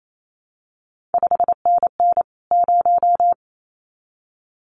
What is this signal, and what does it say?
Morse code, heard as a keyed tone, about 28 wpm, 700 Hz. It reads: 5NN 0
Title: CQ WW DX Contest Exchange Encore - 28wpm